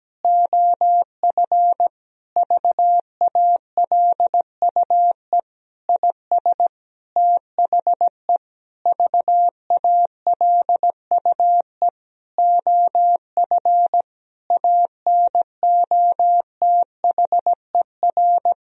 Morse code, heard as a keyed tone, about 17 wpm, 700 Hz. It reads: OF VALUE IS THE VALUE OF ANOTHER
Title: Sentences from Top 1000 Words - Review Part 3 - 17wpm